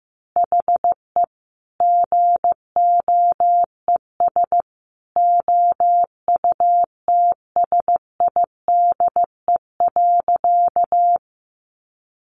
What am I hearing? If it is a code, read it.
Morse code, heard as a keyed tone, about 15 wpm, 700 Hz. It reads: HE GOES OUTSIDE.
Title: Easy Phrases 15wpm